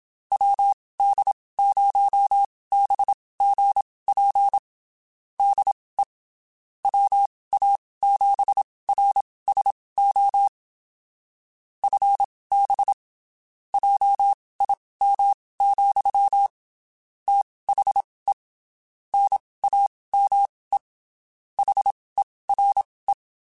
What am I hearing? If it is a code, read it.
Morse code, heard as a keyed tone, about 27 wpm, spaced as for 19 wpm, 800 Hz. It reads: WD0BGP DE WA7RSO FB JIM, THE NAME HERE
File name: (spaced) / QSOs Spaced-mp3s / WD0BGP 1 @25 sp